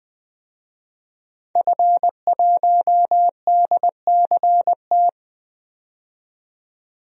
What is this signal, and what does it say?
Morse code, heard as a keyed tone, about 20 wpm, 700 Hz. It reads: F1DCT
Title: DXCC-100 Call Signs - 5 Characters - 20wpm